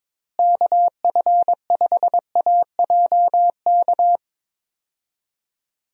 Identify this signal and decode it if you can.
Morse code, heard as a keyed tone, about 22 wpm, 700 Hz. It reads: KF5AJK
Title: USA Call Signs - 6 Characters - Repeated 2x - 22wpm